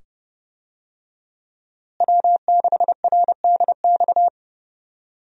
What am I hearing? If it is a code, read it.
Morse code, heard as a keyed tone, about 30 wpm, 700 Hz. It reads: W6RDX
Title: USA Call Signs - 5 Characters - Repeated 2x - 30wpm